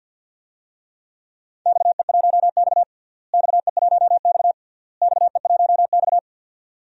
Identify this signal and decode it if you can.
Morse code, heard as a keyed tone, about 50 wpm, 700 Hz. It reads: XE1X XE1X XE1X
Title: DXCC-100 Call Signs - 4 Characters - Repeated 3x - 50wpm